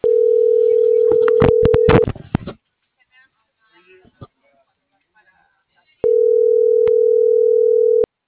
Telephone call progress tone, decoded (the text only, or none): ringback tone